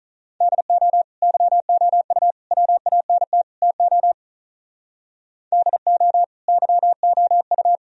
Morse code, heard as a keyed tone, about 44 wpm, 700 Hz. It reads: DO YOU WANT TO DO YOU
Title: Speed-Racing - Sets of 4 Words 35wpm